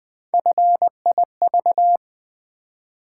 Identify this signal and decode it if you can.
Morse code, heard as a keyed tone, about 20 wpm, 700 Hz. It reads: FIV
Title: Build A Word - 1 Letter At A Time - 4-Letter Words - 20wpm